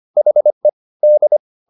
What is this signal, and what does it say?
Morse code, heard as a keyed tone, about 25 wpm, 600 Hz. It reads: HE D
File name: the old man and the sea part 1 25wpm